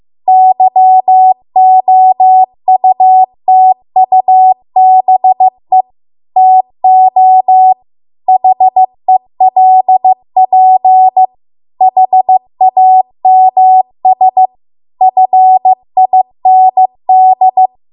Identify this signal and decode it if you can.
Morse code, heard as a keyed tone, about 15 wpm, 750 Hz. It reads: YOUTUBE TO HELP HAMS FIND